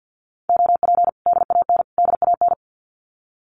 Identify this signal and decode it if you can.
Morse code, heard as a keyed tone, about 50 wpm, 700 Hz. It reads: OP DAN DAN
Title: Common QSO Phrases Encore 50wpm